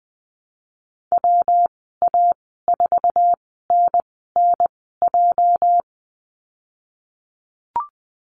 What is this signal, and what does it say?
Morse code, heard as a keyed tone, about 20 wpm, spaced as for 15 wpm, 700 Hz. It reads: WA4NNJ
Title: 238 Lesson 38g - Letter q - Call Signs 20wpm